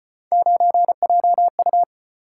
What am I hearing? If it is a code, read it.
Morse code, heard as a keyed tone, about 34 wpm, 700 Hz. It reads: 9JU